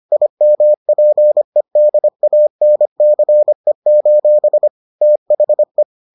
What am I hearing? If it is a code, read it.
Morse code, heard as a keyed tone, about 25 wpm, 600 Hz. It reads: IMPEDANCE: THE